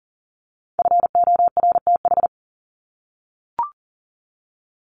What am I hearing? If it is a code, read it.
Morse code, heard as a keyed tone, about 40 wpm, 700 Hz. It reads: FORTH
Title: Top 1751-2000 Words 40wpm